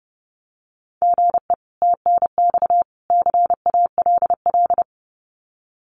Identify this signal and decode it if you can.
Morse code, heard as a keyed tone, about 30 wpm, 700 Hz. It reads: GE TNX CALL
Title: Common QSO Phrases 30wpm